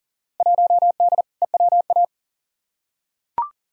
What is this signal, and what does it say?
Morse code, heard as a keyed tone, about 40 wpm, 700 Hz. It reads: 1D EWA
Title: Field Day Exchanges - 40wpm